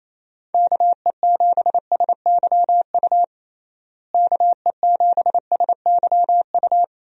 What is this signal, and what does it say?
Morse code, heard as a keyed tone, about 27 wpm, 700 Hz. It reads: KE7SYU KE7SYU
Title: USA Call Signs - 6 Characters - Repeated 2x - 28wpm